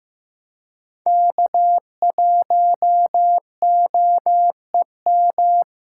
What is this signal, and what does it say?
Morse code, heard as a keyed tone, about 15 wpm, 700 Hz. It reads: K1OEM
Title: USA Call Signs - 4x Character Spacing - 15wpm